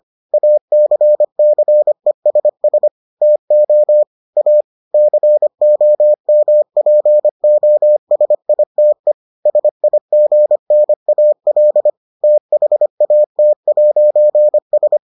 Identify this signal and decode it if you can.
Morse code, heard as a keyed tone, about 25 wpm, 600 Hz. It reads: ACCESS TO A COMPOSITE SIGNAL THAT'S